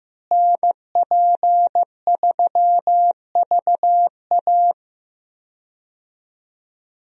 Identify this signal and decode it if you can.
Morse code, heard as a keyed tone, about 15 wpm, 700 Hz. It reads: NP3VA
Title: USA Call Signs - 5 Characters - Repeated 2x - 15wpm